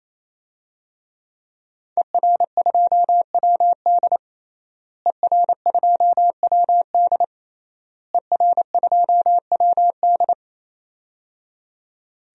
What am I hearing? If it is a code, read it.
Morse code, heard as a keyed tone, about 28 wpm, 700 Hz. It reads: ER2WD ER2WD ER2WD
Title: DXCC-100 Call Signs - 5 Characters - Repeated 3x - 28wpm